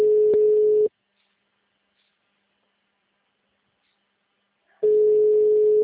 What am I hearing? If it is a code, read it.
Ringback tone (Europe, 425 Hz).
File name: out- -2016- - - .90